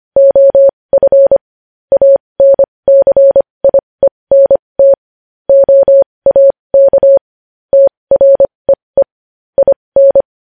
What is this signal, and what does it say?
Morse code, heard as a keyed tone, about 25 wpm, spaced as for 20 wpm, 550 Hz. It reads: OF ANCIENT OAK TREE IN
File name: podcast / News headlines-1x-25x20